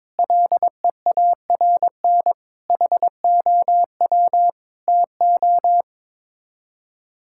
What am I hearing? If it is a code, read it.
Morse code, heard as a keyed tone, about 22 wpm, 700 Hz. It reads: LEARN HOW TO
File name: practice / 22wpm / Sets of 6 Words - 22wpm